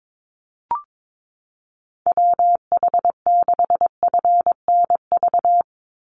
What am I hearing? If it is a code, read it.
Morse code, heard as a keyed tone, about 22 wpm, 700 Hz. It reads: WH6FNV
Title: USA Call Signs 22wpm